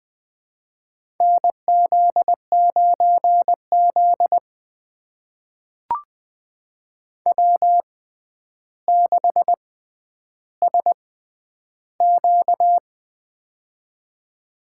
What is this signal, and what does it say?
Morse code, heard as a keyed tone, about 20 wpm, spaced as for 10 wpm, 700 Hz. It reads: NZ9Z W6SQ
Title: USA Call Signs - 4 Characters - 6x Character Spacing - 20wpm